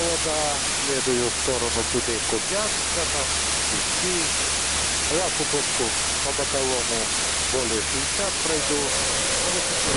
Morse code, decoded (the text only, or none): none